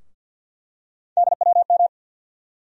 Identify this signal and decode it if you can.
Morse code, heard as a keyed tone, about 51 wpm, 700 Hz. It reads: DWM
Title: USA Call Sign Postfixes - 3 Letters 50wpm